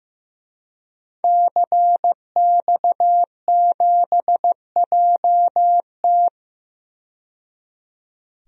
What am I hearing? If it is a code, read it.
Morse code, heard as a keyed tone, about 15 wpm, 700 Hz. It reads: CX7JT
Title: DXCC-100 Call Signs - 5 Characters - 15wpm